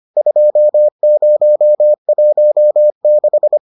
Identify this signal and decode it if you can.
Morse code, heard as a keyed tone, about 25 wpm, 600 Hz. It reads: 2016